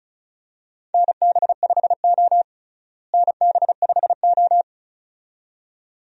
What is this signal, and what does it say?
Morse code, heard as a keyed tone, about 36 wpm, 700 Hz. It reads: NB5O NB5O
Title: USA Call Signs - 4 Characters - Repeated 2x - 35wpm